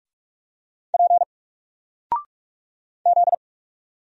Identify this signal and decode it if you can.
Morse code, heard as a keyed tone, about 46 wpm, 700 Hz. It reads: P Z
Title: Single Letter-Number 45wpm